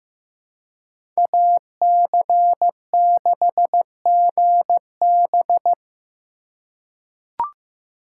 Morse code, heard as a keyed tone, about 16 wpm, 700 Hz. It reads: AC6GB
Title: USA Call Signs - 2x Character Spacing - 15wpm